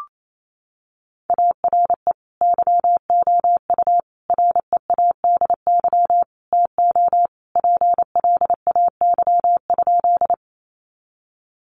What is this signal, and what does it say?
Morse code, heard as a keyed tone, about 28 wpm, 700 Hz. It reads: ARE YOU READY TO PLAY?
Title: Sentences from Top 400 Words 28wpm